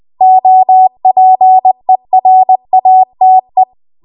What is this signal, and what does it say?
Morse code, heard as a keyed tone, about 20 wpm, 750 Hz. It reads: OPERATE
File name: Morse / 20WPM